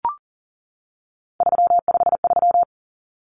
Morse code, heard as a keyed tone, about 40 wpm, 700 Hz. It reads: 353
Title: Numbers - 3-Digit 40wpm